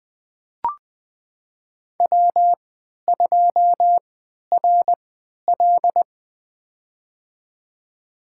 Morse code, heard as a keyed tone, about 20 wpm, spaced as for 11 wpm, 700 Hz. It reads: W2RL
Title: USA Call Signs - 3x Character Spacing - 20wpm